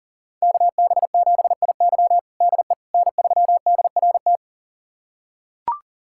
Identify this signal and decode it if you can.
Morse code, heard as a keyed tone, about 40 wpm, 700 Hz. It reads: KB7IY DE N3DRT
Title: USA Callsign DE Callsign - 5 Characters - 40wpm